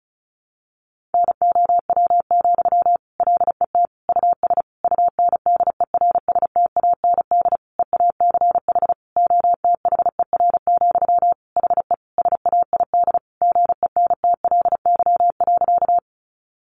Morse code, heard as a keyed tone, about 34 wpm, 700 Hz. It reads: NOW, LET US UNDERSTAND EACH OTHER, HE SAID GENTLY.
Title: Sentences from Top 1500 Words 35wpm